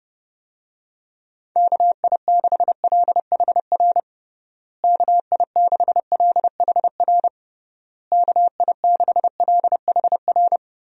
Morse code, heard as a keyed tone, about 30 wpm, 700 Hz. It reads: KI6LHR KI6LHR KI6LHR
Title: USA Call Signs - 6 Characters - Repeated 3x - 30wpm